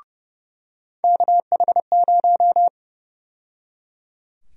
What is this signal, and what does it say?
Morse code, heard as a keyed tone, about 30 wpm, 700 Hz. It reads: KH0
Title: USA Call Sign Prefixes - 30wpm